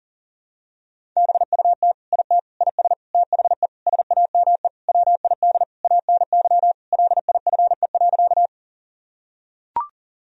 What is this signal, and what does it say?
Morse code, heard as a keyed tone, about 40 wpm, 700 Hz. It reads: BUT IT IS THE SAME WID ANY LIFE.
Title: Sentences from Top 300 Words and Abbreviations 40wpm